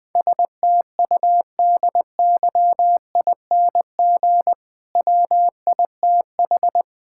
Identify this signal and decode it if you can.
Morse code, heard as a keyed tone, about 20 wpm, 700 Hz. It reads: STUDYING WITH
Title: Difficult Sentences with Hints - 20wpm